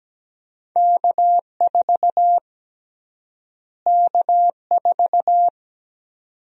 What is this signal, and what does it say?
Morse code, heard as a keyed tone, about 17 wpm, 700 Hz. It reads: K4 K4
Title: USA Call Sign Prefixes - Repeated 2x - 17wpm